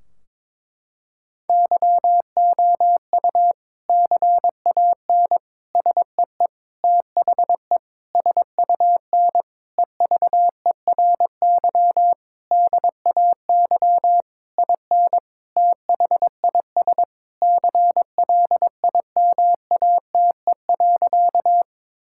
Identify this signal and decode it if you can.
Morse code, heard as a keyed tone, about 22 wpm, 700 Hz. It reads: YOU CAN SEE THE SUN EVERY DAY IN THIS CLIMATE.
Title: Sentences from Top 1600 Words 22wpm